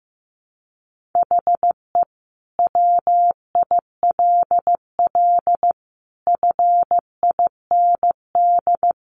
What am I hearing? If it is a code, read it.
Morse code, heard as a keyed tone, about 15 wpm, 700 Hz. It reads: HE WILL FIND
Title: Sentences from Top 1400 Words 15wpm